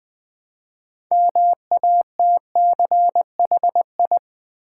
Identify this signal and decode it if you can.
Morse code, heard as a keyed tone, about 20 wpm, 700 Hz. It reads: MATCHI
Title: Build A Word - 2 Letters At A Time - 8-Letter Words - 20wpm